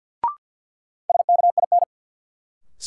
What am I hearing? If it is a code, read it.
Morse code, heard as a keyed tone, about 53 wpm, 700 Hz. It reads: SKIN
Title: Sentences from Top 1300 Words - New Word Drill - 50wpm